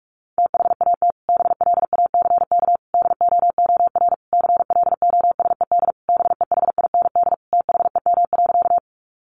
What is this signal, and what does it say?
Morse code, heard as a keyed tone, about 44 wpm, 700 Hz. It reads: THAT BLACK DOOR CLOSED BEHIND THEM.
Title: Sentences from Top 1400 Words 45wpm